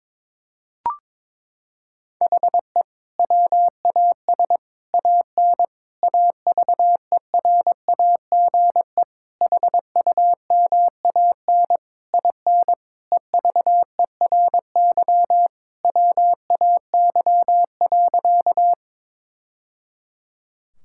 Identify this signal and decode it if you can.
Morse code, heard as a keyed tone, about 22 wpm, 700 Hz. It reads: HE WAS AN AVERAGE HUMAN IN EVERY WAY.
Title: Sentences from Top 1000 Words - Review Part 6 - 22wpm